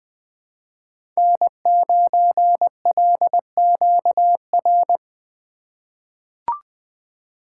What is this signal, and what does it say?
Morse code, heard as a keyed tone, about 20 wpm, 700 Hz. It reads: N9LQR
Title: USA Call Signs - 5 Characters - 2x Character Spacing - 20wpm